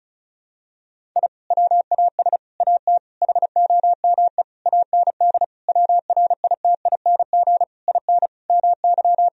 Morse code, heard as a keyed tone, about 36 wpm, 700 Hz. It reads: I WAS AT HOME AND WRITING IN MY
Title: Sentences from Top 1000 Words - Review Part 1 - 35wpm